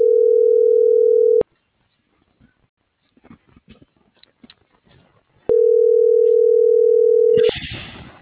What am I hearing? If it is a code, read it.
Ringback tone (North America), 440 + 480 Hz.
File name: CENTRAL SAGOT / out omar.esagot